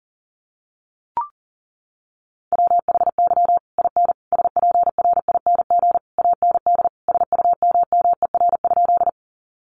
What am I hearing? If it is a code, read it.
Morse code, heard as a keyed tone, about 40 wpm, 700 Hz. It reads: WHY IN SPRING AND SUMMER?
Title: Sentences from Top 1200 Words 40wpm